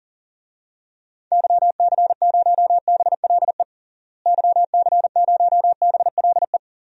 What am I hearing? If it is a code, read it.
Morse code, heard as a keyed tone, about 40 wpm, 700 Hz. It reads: YC0BLE YC0BLE
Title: DXCC-100 Call Signs - 6 Characters - Repeated 3x - 40wpm